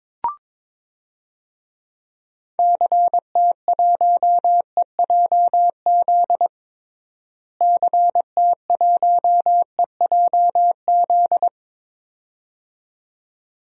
Morse code, heard as a keyed tone, about 22 wpm, 700 Hz. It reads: CT1EJZ CT1EJZ
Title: DXCC-100 Call Signs - 6 Characters - Repeated 2x - 22wpm